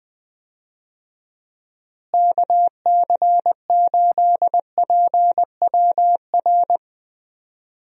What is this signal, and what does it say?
Morse code, heard as a keyed tone, about 20 wpm, 700 Hz. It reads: KC8PWR
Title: USA Call Signs - 6 Characters - Repeated 3x - 20wpm